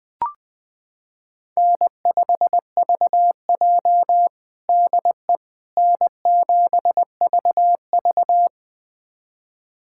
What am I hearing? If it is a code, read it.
Morse code, heard as a keyed tone, about 20 wpm, 700 Hz. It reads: N5VJ DE N7VV
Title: USA Callsign DE Callsign - 4 Characters - 20wpm